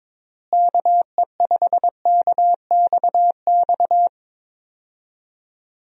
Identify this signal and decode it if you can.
Morse code, heard as a keyed tone, about 21 wpm, 700 Hz. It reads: KE5KXX
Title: USA Call Signs - 6 Characters - Repeated 2x - 22wpm